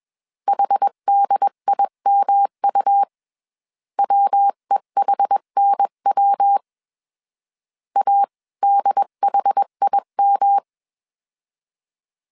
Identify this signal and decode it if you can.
Morse code, heard as a keyed tone, about 21 wpm, 800 Hz. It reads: HDIMU WEHNW ADHIM
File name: CW / DET / 1 3 1500C S B 20wpm 1754